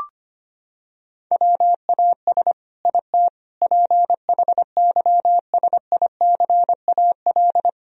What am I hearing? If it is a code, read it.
Morse code, heard as a keyed tone, about 25 wpm, 700 Hz. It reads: WAS IT PHYSICAL
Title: Sentences from Top 1000 Words - Review Part 4 - 25wpm